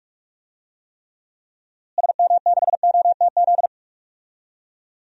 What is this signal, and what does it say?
Morse code, heard as a keyed tone, about 46 wpm, 700 Hz. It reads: SM6OTZ
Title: DXCC-100 Call Signs - 6 Characters - Repeated 2x - 45wpm